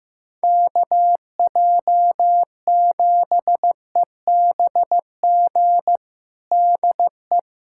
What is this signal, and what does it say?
Morse code, heard as a keyed tone, about 15 wpm, 700 Hz. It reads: KJ7EBG DE
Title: USA Callsign DE Callsign - 6 Characters - 15wpm